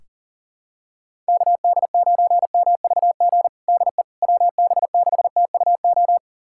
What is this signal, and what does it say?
Morse code, heard as a keyed tone, about 40 wpm, 700 Hz. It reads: KD9MVG DE WB6TUO
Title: USA Callsign DE Callsign 40wpm